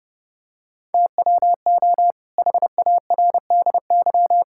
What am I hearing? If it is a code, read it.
Morse code, heard as a keyed tone, about 30 wpm, 700 Hz. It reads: TWO HARDY
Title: Pangram Challenge v2 - 30wpm